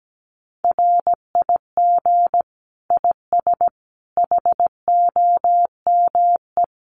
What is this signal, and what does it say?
Morse code, heard as a keyed tone, about 17 wpm, 700 Hz. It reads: RIG IS HOME
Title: Common QSO Phrases 17wpm